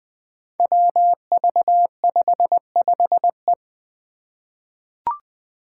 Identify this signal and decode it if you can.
Morse code, heard as a keyed tone, about 20 wpm, 700 Hz. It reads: WV55E
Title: Five Challenging Letter-Number Combinations 20wpm